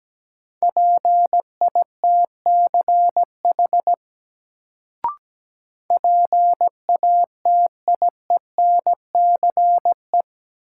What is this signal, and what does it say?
Morse code, heard as a keyed tone, about 17 wpm, 700 Hz. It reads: PITCH PATIENCE
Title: Sentences from Top 1300 Words - New Word Drill Alt - 17wpm